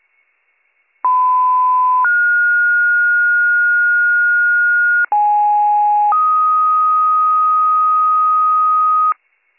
Two-tone page, heard as a one-second tone, then a three-second tone, about 1010 then 1430 Hz, about 830 then 1250 Hz, with noise. It is heard from 1.0 to 9.1 s.